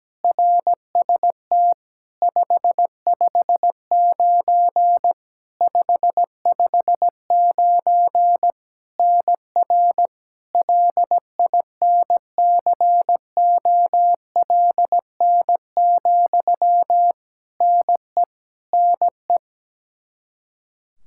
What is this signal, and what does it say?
Morse code, heard as a keyed tone, about 17 wpm, 700 Hz. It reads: RST 559 559 NR LINCOLN, NE NE
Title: Practice QSO - Exchange 1 - Beginning - 17wpm